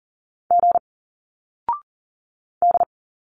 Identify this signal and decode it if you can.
Morse code, heard as a keyed tone, about 39 wpm, 700 Hz. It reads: G D
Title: Single Letters 40wpm